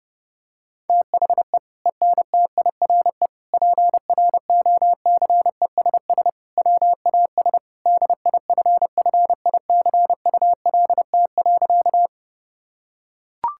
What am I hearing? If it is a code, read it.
Morse code, heard as a keyed tone, about 32 wpm, 700 Hz. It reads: THE ENTIRE PROCESS WAS DIFFICULT.
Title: Sentences from Top 1000 Words - Review Part 5 - 30wpm